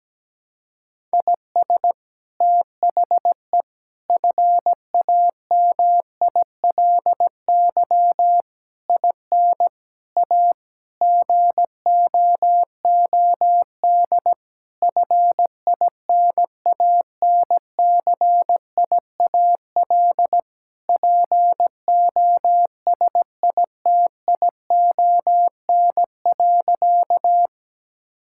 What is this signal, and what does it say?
Morse code, heard as a keyed tone, about 17 wpm, 700 Hz. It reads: IS THE FAMILY IN A GOOD FINANCIAL POSITION.
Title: Sentences from Top 1000 Words - Review Part 5 - 17wpm